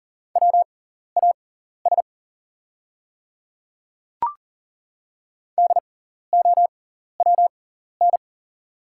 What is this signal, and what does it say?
Morse code, heard as a keyed tone, about 40 wpm, spaced as for 14 wpm, 690 Hz. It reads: WAS DOWN
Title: 6x Character Spacing - Top 100 Words - 40wpm